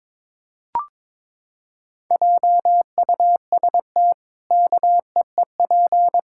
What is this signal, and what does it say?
Morse code, heard as a keyed tone, about 22 wpm, 700 Hz. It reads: JUST KEEP
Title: Pangram Challenge v2 - 22wpm